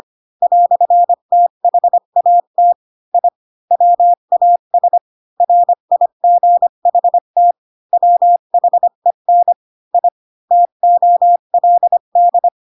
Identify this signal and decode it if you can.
Morse code, heard as a keyed tone, about 25 wpm, 700 Hz. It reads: "THAT I WAS RIGHT WHEN I TOLD
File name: Pinocchio / Pinocchio ch7 20 25 ik1hge